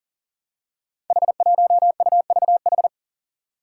Morse code, heard as a keyed tone, about 42 wpm, 700 Hz. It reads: H1UVH
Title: Five Challenging Letter-Number Combinations 40wpm